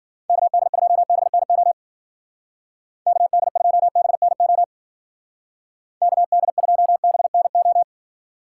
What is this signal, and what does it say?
Morse code, heard as a keyed tone, about 49 wpm, 700 Hz. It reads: KD2BNO KD2BNO KD2BNO
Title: Speed Racing - USA 2x3 Call Signs 40wpm